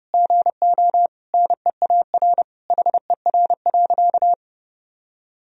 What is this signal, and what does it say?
Morse code, heard as a keyed tone, about 30 wpm, 700 Hz. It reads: GO NEAR HER.
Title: Sentences from Top 1000 Words - Review Part 4 - 30wpm